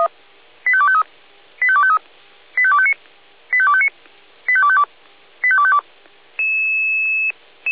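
Five-tone selective call (ZVEI), heard as quick runs of five tones, tones of about 70 ms. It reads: 85253 85253 85259 85259 85252 85252
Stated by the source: ZVEI five-tone selective call received via SDR.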